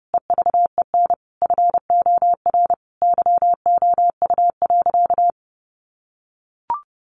Morse code, heard as a keyed tone, about 30 wpm, 700 Hz. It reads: EVEN FOR YOU.
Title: Sentences from Top 700 Words 30wpm